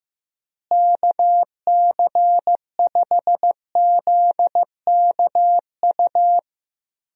Morse code, heard as a keyed tone, about 15 wpm, 700 Hz. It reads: KC5ZKU
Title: USA Call Signs - 6 Characters - Repeated 3x - 15wpm